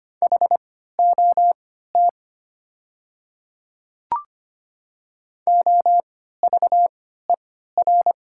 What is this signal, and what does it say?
Morse code, heard as a keyed tone, about 25 wpm, spaced as for 14 wpm, 700 Hz. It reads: HOT OVER
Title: 3x Character Spacing - Top 300 Words - 25wpm